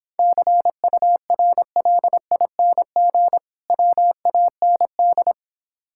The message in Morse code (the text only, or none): CURLING WAND